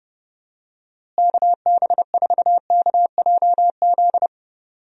Morse code, heard as a keyed tone, about 31 wpm, 700 Hz. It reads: KB4KJZ
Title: USA Call Signs - 6 Characters - Repeated 3x - 30wpm